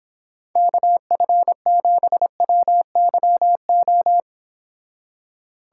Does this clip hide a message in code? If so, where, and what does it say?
Yes: Morse code from 0.6 to 4.2 s, heard as a keyed tone, about 26 wpm, 700 Hz. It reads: KF7WYO